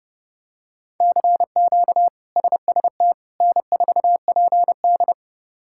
Morse code, heard as a keyed tone, about 32 wpm, 700 Hz. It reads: CQ SST N4PD
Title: K1USN SST QSO Practice - 30wpm